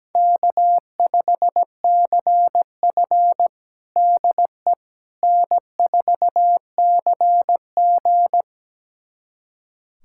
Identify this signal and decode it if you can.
Morse code, heard as a keyed tone, about 17 wpm, 700 Hz. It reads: K5CF DE N4CG
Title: USA Callsign DE Callsign - 4 Characters - 17wpm